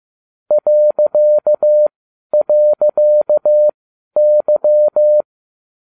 Morse code, heard as a keyed tone, about 15 wpm, spaced as for 11 wpm, 600 Hz. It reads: ..Y
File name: cw / ubacw20 2